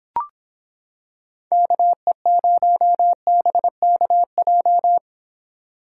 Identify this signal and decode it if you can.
Morse code, heard as a keyed tone, about 26 wpm, 700 Hz. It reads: KE0BKJ